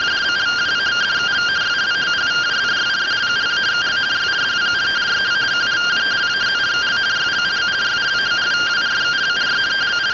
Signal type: two-tone teleprinter signal

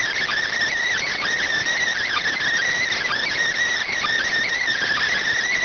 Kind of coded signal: Emergency Alert System (SAME) tones

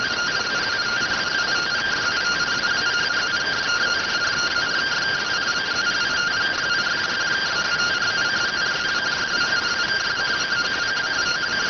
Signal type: two-tone teleprinter signal

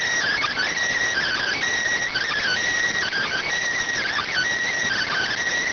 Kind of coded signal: Emergency Alert System (SAME) tones

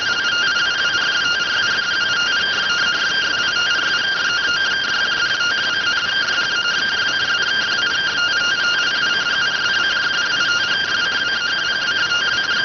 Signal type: two-tone teleprinter signal

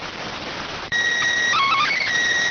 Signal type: SSTV (slow-scan television)

